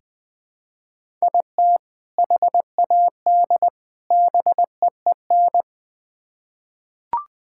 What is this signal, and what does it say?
Morse code, heard as a keyed tone, about 20 wpm, 700 Hz. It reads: IT HAD BEEN
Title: Build A Phrase - 3-Word Phrases - 20wpm